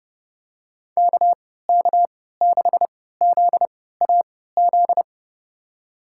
Morse code, heard as a keyed tone, about 30 wpm, spaced as for 17 wpm, 700 Hz. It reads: KK6ZAZ
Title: USA Call Signs - 3x Character Spacing - 30wpm